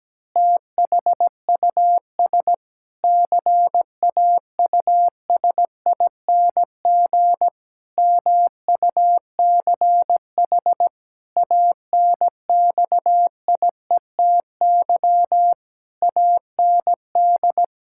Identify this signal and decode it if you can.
Morse code, heard as a keyed tone, about 17 wpm, 700 Hz. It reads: THUS CAUSING MUCH ANXIETY AND
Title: Pangram Challenge v2 - 17wpm